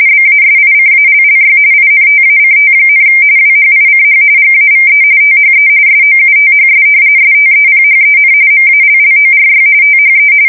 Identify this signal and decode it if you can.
RTTY, about 45.45 baud. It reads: BASICALLY INTACT, WHICH GAVE THE SOUTHS STEPHEN MALLORY THE
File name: fsk170 / usnavy3